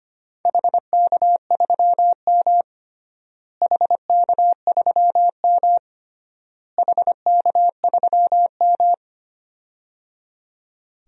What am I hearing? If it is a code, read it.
Morse code, heard as a keyed tone, about 25 wpm, 700 Hz. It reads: HK3M HK3M HK3M
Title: DXCC-100 Call Signs - 4 Characters - Repeated 3x - 25wpm